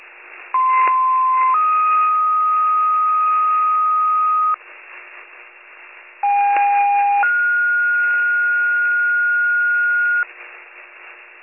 Two-tone page, heard as a one-second tone, then a three-second tone, about 1040 then 1250 Hz, about 810 then 1430 Hz. It is heard from 0.5 to 10.2 s.